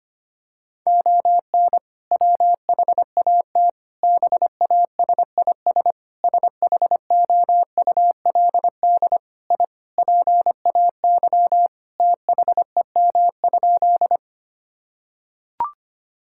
Morse code, heard as a keyed tone, about 25 wpm, 700 Hz. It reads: ON WHAT BASIS SHOULD I PAY THEM?
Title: Sentences from Top 1000 Words - Review Part 5 - 25wpm